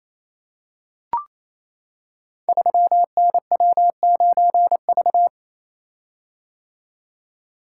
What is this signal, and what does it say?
Morse code, heard as a keyed tone, about 28 wpm, 700 Hz. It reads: 3NW9V
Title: Code Groups - 5 Letter:Numbers - 28wpm